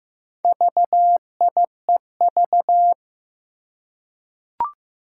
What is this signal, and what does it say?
Morse code, heard as a keyed tone, about 15 wpm, 700 Hz. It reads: VIEV
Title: Four Challenging Letter-Number Combinations 15wpm